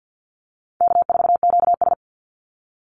Morse code, heard as a keyed tone, about 50 wpm, 700 Hz. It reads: K4QS
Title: NA SOTA Chaser Callsigns 50wpm